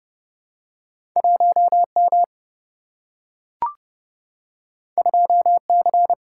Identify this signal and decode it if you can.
Morse code, heard as a keyed tone, about 30 wpm, 700 Hz. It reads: 1 M 2 C
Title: Single-Letter Single-Number 30wpm